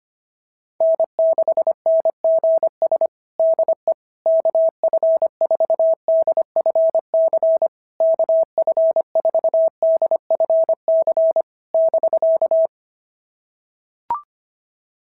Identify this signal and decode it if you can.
Morse code, heard as a keyed tone, about 22 wpm, 650 Hz. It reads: N6NGS DE KF4DFC KF4DFC <BK>
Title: SOTA QSO Practice 25wpm